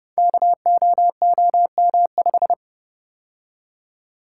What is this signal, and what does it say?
Morse code, heard as a keyed tone, about 30 wpm, 700 Hz. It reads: KOOM5
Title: Code Groups - 5 Letter:Numbers - 30wpm